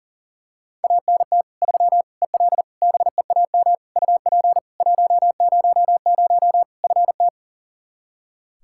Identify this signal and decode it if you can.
Morse code, heard as a keyed tone, about 37 wpm, 700 Hz. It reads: ANT 3 EL BEAM UP 100 FT
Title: K1USN SST QSO Practice 40wpm